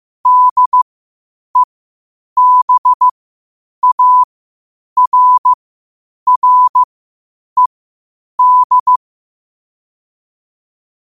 Morse code, heard as a keyed tone, about 15 wpm, spaced as for 8 wpm, 1000 Hz. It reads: DEBARRED
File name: 8wpm